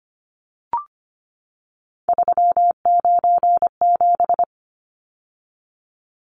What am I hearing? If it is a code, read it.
Morse code, heard as a keyed tone, about 25 wpm, 700 Hz. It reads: 397
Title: Numbers - 3-Digit 25wpm